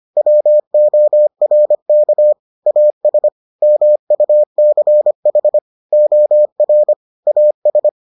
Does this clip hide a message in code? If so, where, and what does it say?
Yes: Morse code from 0.2 to 7.9 s, heard as a keyed tone, about 25 wpm, 600 Hz. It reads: WORK AS MUCH OR AS